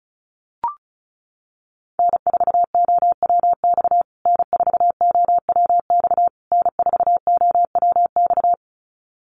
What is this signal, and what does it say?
Morse code, heard as a keyed tone, about 34 wpm, 700 Hz. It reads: N4OWX N4OWX N4OWX
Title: USA Call Signs Repeated 3x 35wpm